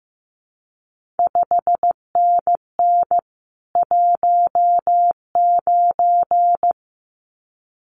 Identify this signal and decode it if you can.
Morse code, heard as a keyed tone, about 15 wpm, 700 Hz. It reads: 5NN 19
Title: CQ WW DX Contest Exchange Encore - 15wpm